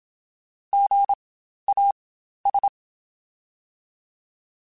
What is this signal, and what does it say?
Morse code, heard as a keyed tone, about 27 wpm, spaced as for 12 wpm, 800 Hz. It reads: GAS
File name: Spaced Alphas / Spaced Alphas mp3s / Gs sp @25